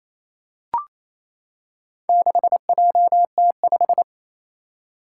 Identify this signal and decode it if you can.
Morse code, heard as a keyed tone, about 27 wpm, 700 Hz. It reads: 6JT5
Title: Four Challenging Letter-Number Combinations 28wpm